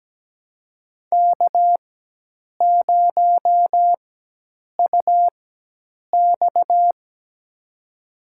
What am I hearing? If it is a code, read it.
Morse code, heard as a keyed tone, about 17 wpm, spaced as for 8 wpm, 700 Hz. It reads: K0UX
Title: USA Call Signs - 4 Characters - 4x Character Spacing - 17wpm